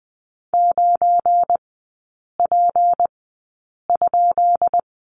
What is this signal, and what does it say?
Morse code, heard as a keyed tone, about 20 wpm, spaced as for 8 wpm, 700 Hz. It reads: 9P?
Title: 172 Lesson 28e - Number 9 - Three Characters 20wpm